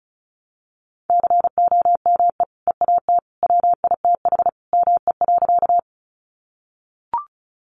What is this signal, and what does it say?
Morse code, heard as a keyed tone, about 34 wpm, 700 Hz. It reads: COME EAT WITH ME.
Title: Sentences from Top 300 Words 35wpm